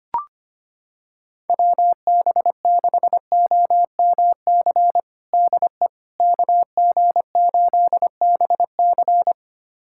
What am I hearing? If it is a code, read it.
Morse code, heard as a keyed tone, about 25 wpm, 700 Hz. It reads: WB6OMC DE KG8BC
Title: USA Callsign DE Callsign 25wpm